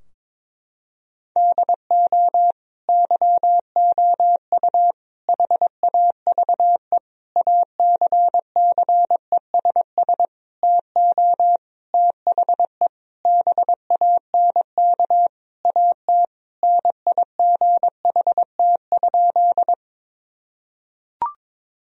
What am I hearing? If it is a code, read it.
Morse code, heard as a keyed tone, about 22 wpm, 700 Hz. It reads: DO YOU HAVE ACCESS TO THE BANK AT NIGHT?
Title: Sentences from Top 1000 Words - Review Part 4 - 22wpm